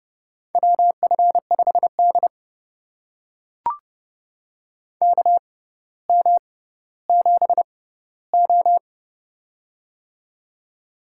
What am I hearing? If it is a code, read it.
Morse code, heard as a keyed tone, about 30 wpm, spaced as for 15 wpm, 700 Hz. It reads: WF5D KM7O
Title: USA Call Signs - 4 Characters - 6x Character Spacing - 30wpm